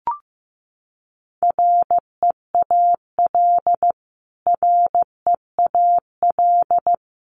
Morse code, heard as a keyed tone, about 15 wpm, 700 Hz. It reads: REAL REAL
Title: Top 500 Words Repeated 15wpm